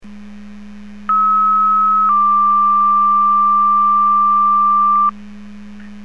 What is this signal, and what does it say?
Two-tone page, heard as a one-second tone, then a three-second tone, about 1290 then 1180 Hz. It is heard from 1.1 to 5.1 s.